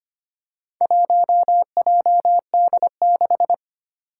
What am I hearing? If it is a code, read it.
Morse code, heard as a keyed tone, about 25 wpm, 700 Hz. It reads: 1JD6
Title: Four Challenging Letter-Number Combinations 25wpm